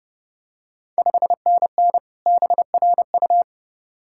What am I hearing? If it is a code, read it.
Morse code, heard as a keyed tone, about 30 wpm, 690 Hz. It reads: 5NN BRU
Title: Ontario QSO Party Exchange 30wpm